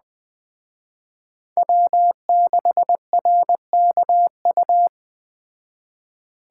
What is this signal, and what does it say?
Morse code, heard as a keyed tone, about 20 wpm, 700 Hz. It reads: W6RKU
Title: Speed Racing - USA 1x3 Call Signs 17wpm